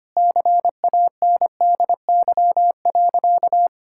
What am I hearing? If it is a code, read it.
Morse code, heard as a keyed tone, about 25 wpm, 700 Hz. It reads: CANDY.
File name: practice / 25wpm / Sentences with 4-Word Phrases - 25wpm